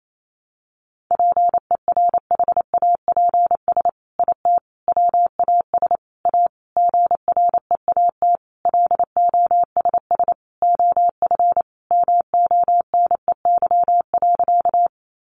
Morse code, heard as a keyed tone, about 28 wpm, 700 Hz. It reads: PERHAPS IT WAS A GREAT LOSS OF MONEY.
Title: Sentences from Top 1400 Words 28wpm